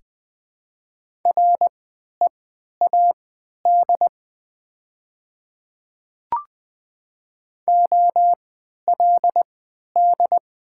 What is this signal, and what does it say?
Morse code, heard as a keyed tone, about 20 wpm, spaced as for 11 wpm, 700 Hz. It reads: READ OLD
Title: 3x Character Spacing - Top 200 Words - 20wpm